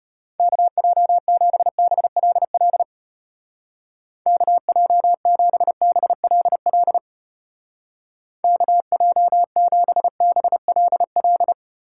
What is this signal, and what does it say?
Morse code, heard as a keyed tone, about 31 wpm, 700 Hz. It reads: KJ7BLL KJ7BLL KJ7BLL